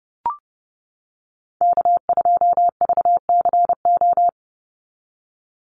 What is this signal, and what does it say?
Morse code, heard as a keyed tone, about 30 wpm, 700 Hz. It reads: K2VCO
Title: Call Signs - 30wpm